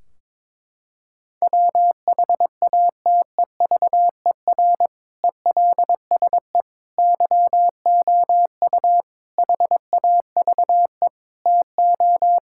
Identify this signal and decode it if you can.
Morse code, heard as a keyed tone, about 22 wpm, 700 Hz. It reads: WHATEVER ELSE YOU HAVE TO
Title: Sentences from Top 1600 Words 22wpm